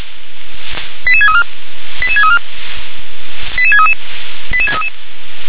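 Five-tone selective call (ZVEI), heard as quick runs of five tones, tones of about 70 ms. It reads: 88634 88634 88633 88633
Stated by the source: ZVEI five-tone selective call received by SDR.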